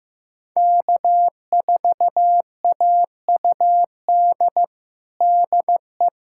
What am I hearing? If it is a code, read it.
Morse code, heard as a keyed tone, about 15 wpm, 700 Hz. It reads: K4AUD DE
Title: USA Callsign DE Callsign - 5 Characters - 15wpm